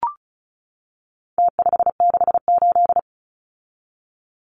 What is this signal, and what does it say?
Morse code, heard as a keyed tone, about 34 wpm, 700 Hz. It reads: T568
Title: Numbers - Contest Style - 4-Digit 35wpm